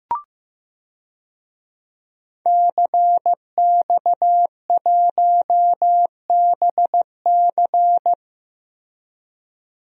Morse code, heard as a keyed tone, about 15 wpm, 700 Hz. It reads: CX1BC
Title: DXCC-100 Call Signs - 5 Characters - Repeated 2x - 15wpm